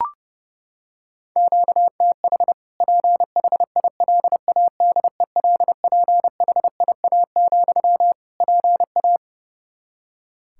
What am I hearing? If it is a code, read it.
Morse code, heard as a keyed tone, about 30 wpm, 700 Hz. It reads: QTH PHILADELPHIA, PA
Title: Practice QSO - Exchange 1 - Beginning - 30wpm